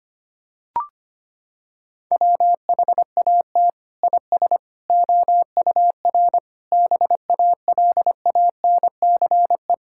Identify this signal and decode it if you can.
Morse code, heard as a keyed tone, about 25 wpm, 700 Hz. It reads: WHAT IS OUR BALANCE
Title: Sentences from Top 1000 Words - Review Part 6 - 25wpm